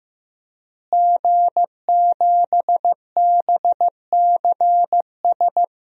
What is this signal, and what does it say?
Morse code, heard as a keyed tone, about 15 wpm, 700 Hz. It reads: G7BCS
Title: DXCC-100 Call Signs - 5 Characters - Repeated 3x - 15wpm